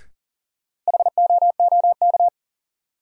Morse code, heard as a keyed tone, about 40 wpm, 700 Hz. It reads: HOOK
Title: Sentences from Top 1600 Words - New Word Drill - 40wpm